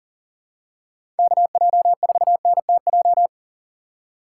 Morse code, heard as a keyed tone, about 40 wpm, 700 Hz. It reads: KJ4NTJ
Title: USA Call Signs - 6 Characters - 6x Character Spacing - 40wpm